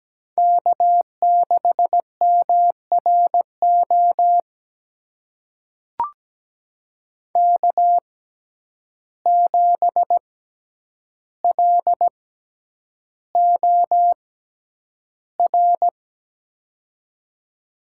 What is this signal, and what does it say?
Morse code, heard as a keyed tone, about 17 wpm, spaced as for 9 wpm, 700 Hz. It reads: K6MRO K7LOR
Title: USA Call Signs - 5 Characters - 6x Character Spacing - 17wpm